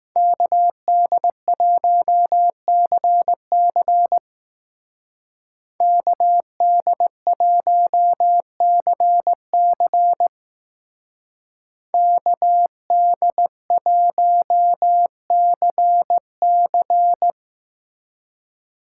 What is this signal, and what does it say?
Morse code, heard as a keyed tone, about 18 wpm, 700 Hz. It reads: KD1CC KD1CC KD1CC
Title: Speed Racing - USA Call Signs 15wpm